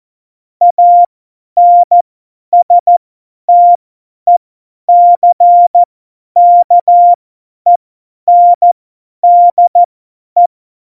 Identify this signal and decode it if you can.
Morse code, heard as a keyed tone, about 14 wpm, spaced as for 10 wpm, 700 Hz. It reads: ANSTECKENDE
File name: Morsekurs / Witze / Witz 06. Oktober 2020, 50 F70